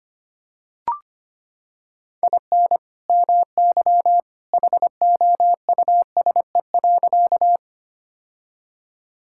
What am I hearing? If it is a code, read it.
Morse code, heard as a keyed tone, about 25 wpm, 700 Hz. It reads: IN MY HOUSE.
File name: practice / 25wpm / Sentences with 2-Word Phrases - 25wpm